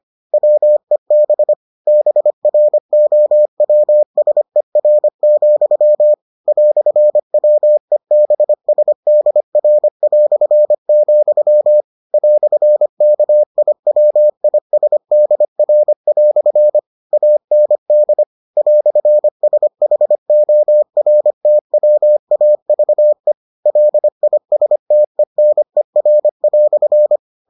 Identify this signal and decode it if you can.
Morse code, heard as a keyed tone, about 25 wpm, 600 Hz. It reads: WEB BROWSER, "WEBSDR", "KIWISDR" AND "SHORTWAVE LISTENER"